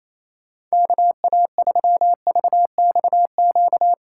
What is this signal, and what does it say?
Morse code, heard as a keyed tone, about 34 wpm, 700 Hz. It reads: KA3VXQ
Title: Speed Racing - USA Call Signs 28wpm